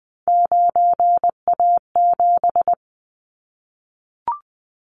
Morse code, heard as a keyed tone, about 19 wpm, 700 Hz. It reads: 9A7
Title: Callsign Prefixes for Top 25 DXCC - 20wpm